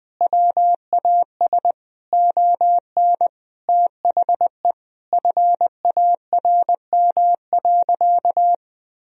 Morse code, heard as a keyed tone, about 21 wpm, 700 Hz. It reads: WAS ON THE FARM.
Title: Sentences from Top 1000 Words - Review Part 2 - 20wpm